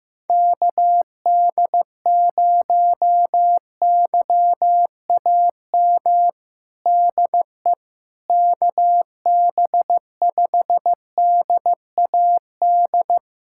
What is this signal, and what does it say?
Morse code, heard as a keyed tone, about 15 wpm, 700 Hz. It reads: KD0YAM DE KB5DAD
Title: USA Callsign DE Callsign - 6 Characters - 15wpm